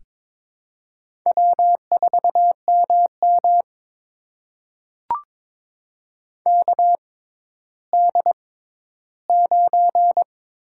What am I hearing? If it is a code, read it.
Morse code, heard as a keyed tone, about 22 wpm, spaced as for 13 wpm, 700 Hz. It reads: W4MM KD9
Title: USA Call Signs - 4 Characters - 6x Character Spacing - 22wpm